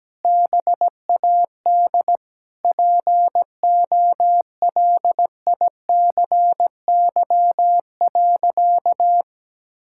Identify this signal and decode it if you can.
Morse code, heard as a keyed tone, about 17 wpm, 700 Hz. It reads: BAD POLICY.
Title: Sentences from Top 1000 Words - Review Part 5 - 17wpm